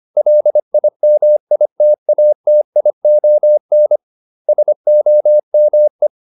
Morse code, heard as a keyed tone, about 25 wpm, 600 Hz. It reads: LIMITATION SOME